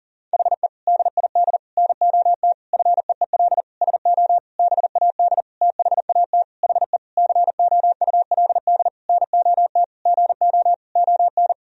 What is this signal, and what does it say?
Morse code, heard as a keyed tone, about 37 wpm, 700 Hz. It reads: HE DID NOT FEEL SO BAD THAT HE COULD NOT GO ON
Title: Sentences from Top 1000 Words - Review Part 4 - 40wpm